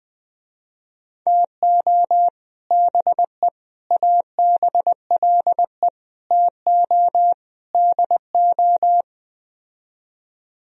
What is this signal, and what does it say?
Morse code, heard as a keyed tone, about 20 wpm, 700 Hz. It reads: TO BE ABLE TO DO
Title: Speed-Racing - Sets of 5 Words 17wpm